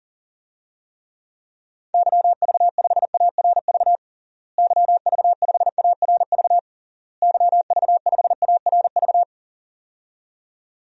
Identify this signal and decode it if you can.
Morse code, heard as a keyed tone, about 40 wpm, 700 Hz. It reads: YV5ARV YV5ARV YV5ARV
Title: DXCC-100 Call Signs - 6 Characters - Repeated 3x - 40wpm